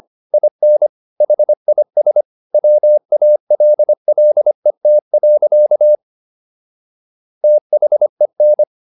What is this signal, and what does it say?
Morse code, heard as a keyed tone, about 25 wpm, 600 Hz. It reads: IN HIS WALLET. THEN